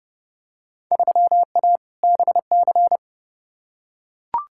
Morse code, heard as a keyed tone, about 30 wpm, 700 Hz. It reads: 3A BC
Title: Field Day Exchanges - 30wpm